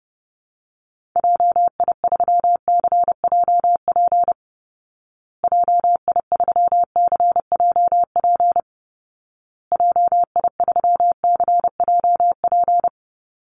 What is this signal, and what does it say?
Morse code, heard as a keyed tone, about 30 wpm, 700 Hz. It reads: JI3CJP JI3CJP JI3CJP
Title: Call Signs Repeated 2x - 30wpm